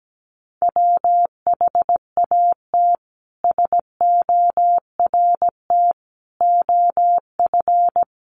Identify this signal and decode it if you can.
Morse code, heard as a keyed tone, about 17 wpm, 700 Hz. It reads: WHAT SORT OF
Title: A Sound Of Thunder 17wpm